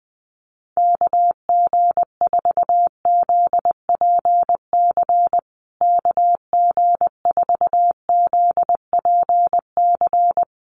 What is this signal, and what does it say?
Morse code, heard as a keyed tone, about 20 wpm, 700 Hz. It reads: KG4ZPC KG4ZPC
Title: USA Call Signs Repeated 2x 20wpm